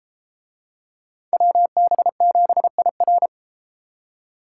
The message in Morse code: WB7IR

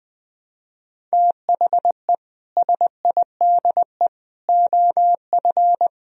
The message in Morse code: THE SIDE OF

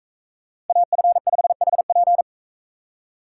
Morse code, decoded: AF5HP

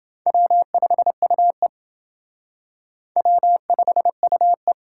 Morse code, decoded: W5UE W5UE

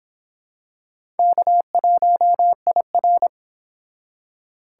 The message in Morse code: K1IR

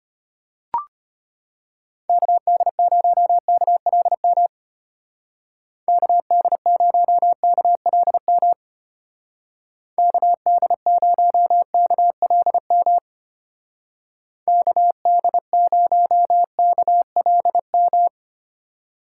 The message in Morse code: KD0KLM KD0KLM KD0KLM KD0KLM

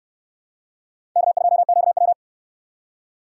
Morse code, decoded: K2QW

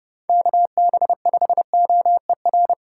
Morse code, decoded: KB5OER

